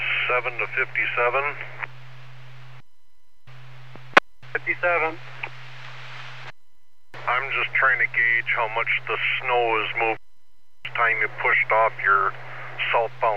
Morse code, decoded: none

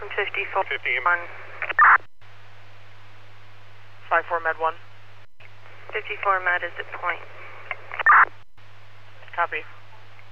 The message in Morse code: none

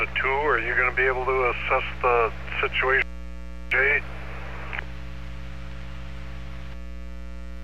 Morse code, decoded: none

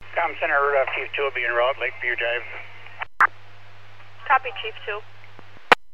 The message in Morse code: none

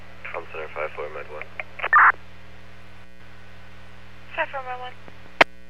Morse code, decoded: none

